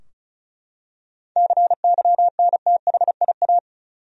Morse code, heard as a keyed tone, about 34 wpm, 700 Hz. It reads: CYNTHIA
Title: USA Common Names - 35wpm